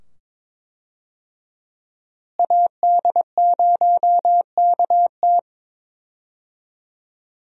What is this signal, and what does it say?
Morse code, heard as a keyed tone, about 22 wpm, 700 Hz. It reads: AD0KT
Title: USA Call Signs - 5 Characters - Repeated 2x - 22wpm